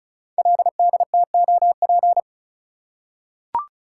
Morse code, heard as a keyed tone, about 36 wpm, 690 Hz. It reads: LDTOP
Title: Code Groups - 5 Letters - 35wpm